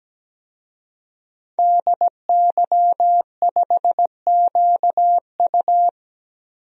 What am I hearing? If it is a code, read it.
Morse code, heard as a keyed tone, about 17 wpm, 700 Hz. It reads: DY5QU
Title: DXCC-100 Call Signs - 5 Characters - 17wpm